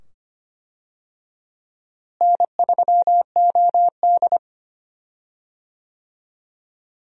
Morse code, heard as a keyed tone, about 25 wpm, 700 Hz. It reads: N3OD
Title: USA Call Signs - 4 Characters - Repeated 2x - 25wpm